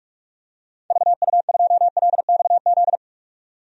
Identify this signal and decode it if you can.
Morse code, heard as a keyed tone, about 46 wpm, 700 Hz. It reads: VU2LXZ
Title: DXCC-100 Call Signs - 6 Characters - 45wpm